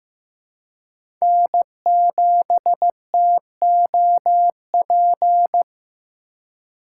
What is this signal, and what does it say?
Morse code, heard as a keyed tone, about 15 wpm, 700 Hz. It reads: N7TOP
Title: USA Call Signs - 5 Characters - 4x Character Spacing - 15wpm